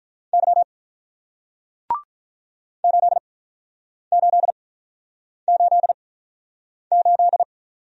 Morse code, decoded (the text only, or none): Y 8888